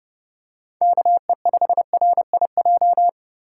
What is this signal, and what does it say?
Morse code, heard as a keyed tone, about 30 wpm, 700 Hz. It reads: KE5RIJ
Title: USA Call Signs - 6 Characters - 2x Character Spacing - 30wpm